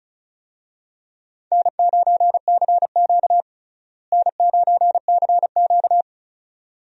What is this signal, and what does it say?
Morse code, heard as a keyed tone, about 36 wpm, 700 Hz. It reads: N9CQ N9CQ
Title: USA Call Signs - 4 Characters - Repeated 2x - 35wpm